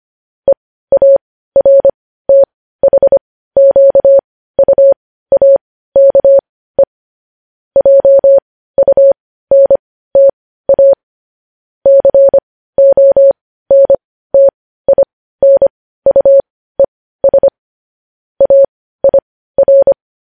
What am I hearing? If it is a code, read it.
Morse code, heard as a keyed tone, about 25 wpm, spaced as for 15 wpm, 550 Hz. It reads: EARTHQUAKE JUNTA CONTINUES AIR
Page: News Headlines in Morse Code at 15 WPM